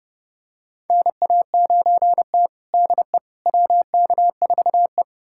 Morse code, heard as a keyed tone, about 31 wpm, 700 Hz. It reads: NA9T DE WK4E
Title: USA Callsign DE Callsign - 4 Characters - 30wpm